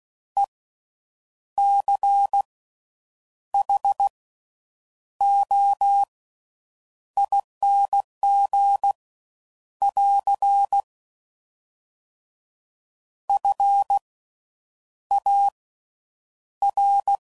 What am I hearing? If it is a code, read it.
Morse code, heard as a keyed tone, about 16 wpm, spaced as for 7 wpm, 800 Hz. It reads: ECHOING <AR> FAR
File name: Slow With Faster Ings